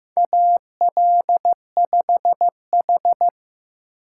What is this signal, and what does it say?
Morse code, heard as a keyed tone, about 15 wpm, 700 Hz. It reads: AL5H
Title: USA Call Signs - 4 Characters - 8x Character Spacing - 15wpm